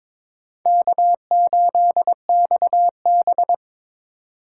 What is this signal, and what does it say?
Morse code, heard as a keyed tone, about 26 wpm, 700 Hz. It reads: K8XB
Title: Speed Racing - USA 1x2 Call Signs 22wpm